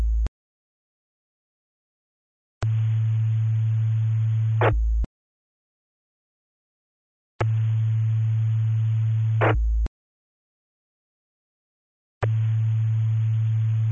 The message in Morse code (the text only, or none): none